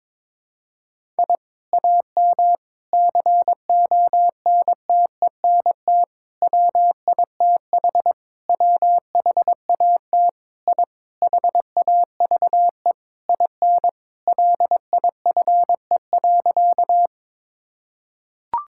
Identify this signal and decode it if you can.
Morse code, heard as a keyed tone, about 21 wpm, 700 Hz. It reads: I AM CONTENT WITH WHAT I HAVE IN LIFE.
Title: Sentences from Top 1000 Words - Review Part 6 - 22wpm